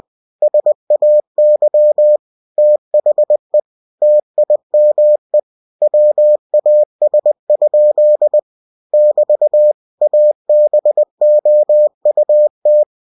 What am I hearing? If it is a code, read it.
Morse code, heard as a keyed tone, about 20 wpm, 600 Hz. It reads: SAY THE TIME WAS? <BT> ABOUT
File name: cw-books / winnie-the-pooh / winnie-20wpm-page-0024